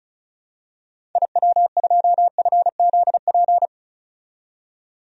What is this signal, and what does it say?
Morse code, heard as a keyed tone, about 36 wpm, 700 Hz. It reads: IW2FZP
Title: DXCC-100 Call Signs - 6 Characters - 35wpm